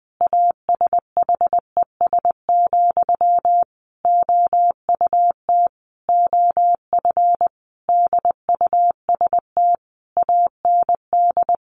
Morse code, heard as a keyed tone, about 20 wpm, 700 Hz. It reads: ASHES, OUT OF DUST AND
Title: A Sound Of Thunder 20wpm